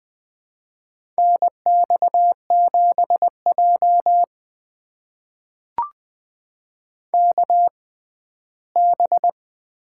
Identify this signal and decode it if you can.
Morse code, heard as a keyed tone, about 20 wpm, spaced as for 14 wpm, 700 Hz. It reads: NX7J KB
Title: USA Call Signs - 6x Character Spacing - 20wpm